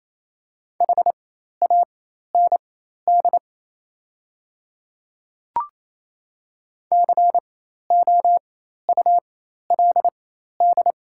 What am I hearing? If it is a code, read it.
Morse code, heard as a keyed tone, about 27 wpm, spaced as for 13 wpm, 690 Hz. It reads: HAND COULD
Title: 4x Character Spacing - Top 200 Words - 28wpm